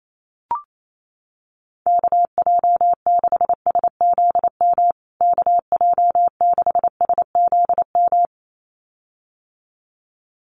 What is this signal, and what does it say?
Morse code, heard as a keyed tone, about 28 wpm, 700 Hz. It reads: KJ6SZM KJ6SZM
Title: USA Call Signs Repeated 2x 28wpm